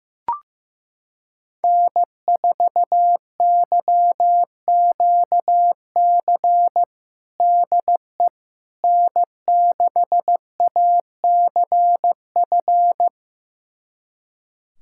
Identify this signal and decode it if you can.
Morse code, heard as a keyed tone, about 15 wpm, 700 Hz. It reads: N4YQC DE N6ACF
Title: USA Callsign DE Callsign - 5 Characters - 15wpm